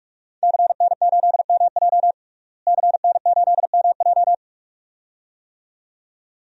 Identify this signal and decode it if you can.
Morse code, heard as a keyed tone, about 43 wpm, 700 Hz. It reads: CN8MJ CN8MJ
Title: DXCC-100 Call Signs - 5 Characters - Repeated 2x - 45wpm